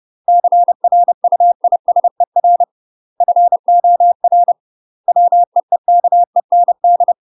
Morse code, heard as a keyed tone, about 30 wpm, 700 Hz. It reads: CRUISER FOR WEEKEND